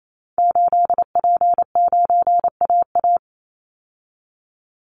Morse code, heard as a keyed tone, about 28 wpm, 700 Hz. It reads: 8P9AA
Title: Call Signs Repeated 3x - 28wpm